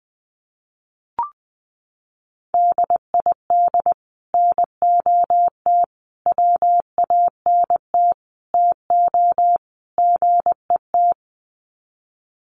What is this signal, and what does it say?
Morse code, heard as a keyed tone, about 19 wpm, 700 Hz. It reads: DID NOT WANT TO GET
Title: Sets of 5 Words Encore2 20wpm